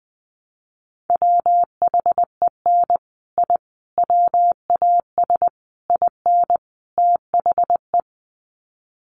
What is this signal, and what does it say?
Morse code, heard as a keyed tone, about 20 wpm, 700 Hz. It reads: WHEN I WAS IN THE
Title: Sets of 5 Words Encore2 20wpm